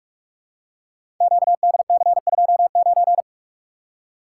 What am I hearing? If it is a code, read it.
Morse code, heard as a keyed tone, about 44 wpm, 700 Hz. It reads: QDC29
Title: Code Groups - 5 Letter:Numbers - 45wpm